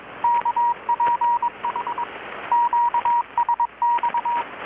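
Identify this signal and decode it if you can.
Morse code, heard as a keyed tone, about 22 wpm, 950 Hz, with radio noise. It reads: KPH QSX
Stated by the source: Morse from KPH's Night of Nights XX maritime radiotelegraphy event, received in Chiapas, Mexico about 3,800 km away on a Kenwood R-600 with a grounded 10 m long-wire antenna.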